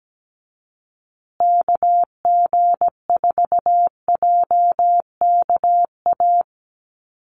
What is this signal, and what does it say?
Morse code, heard as a keyed tone, about 17 wpm, 700 Hz. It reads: KG4JKA
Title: USA Call Signs Repeated 2x 17wpm